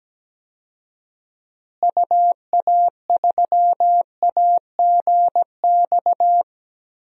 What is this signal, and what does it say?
Morse code, heard as a keyed tone, about 17 wpm, 700 Hz. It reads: UA3AGX
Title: DXCC-100 Call Signs - 6 Characters - Repeated 3x - 17wpm